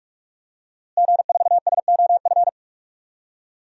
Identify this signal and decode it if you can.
Morse code, heard as a keyed tone, about 45 wpm, 700 Hz. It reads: G4SOF
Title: DXCC-100 Call Signs - 5 Characters - Repeated 2x - 45wpm